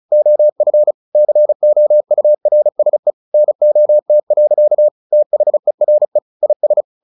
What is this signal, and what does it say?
Morse code, heard as a keyed tone, about 35 wpm, 600 Hz. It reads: OF COURSE NOT. THERE IS
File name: the old man and the sea 2 35wpm